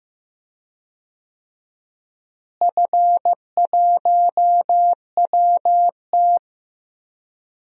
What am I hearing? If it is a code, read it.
Morse code, heard as a keyed tone, about 15 wpm, 700 Hz. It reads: F1WT
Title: DXCC-100 Call Signs - 4 Characters - Repeated 3x - 15wpm